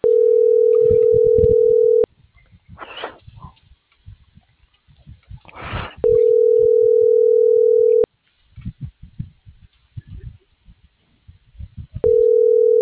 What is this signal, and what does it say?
Ringback tone (North America), 440 + 480 Hz.